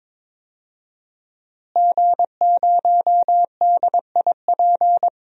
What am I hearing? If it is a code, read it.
Morse code, heard as a keyed tone, about 21 wpm, 700 Hz. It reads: G0DIP
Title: DXCC-100 Call Signs - 5 Characters - Repeated 2x - 22wpm